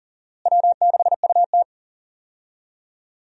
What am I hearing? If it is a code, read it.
Morse code, heard as a keyed tone, about 40 wpm, 700 Hz. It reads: W6UT
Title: USA Call Signs - 4 Characters - Repeated 2x - 40wpm